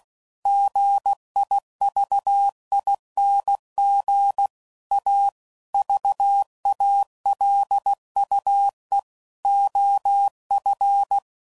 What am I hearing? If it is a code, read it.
Morse code, heard as a keyed tone, about 16 wpm, 800 Hz. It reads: GIVING A VALUE OF
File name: CW Code Groups @15WPM / Technical Subjects / XL